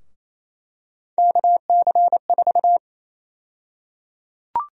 Morse code, heard as a keyed tone, about 28 wpm, 700 Hz. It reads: KC4
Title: USA Call Sign Prefixes - 28wpm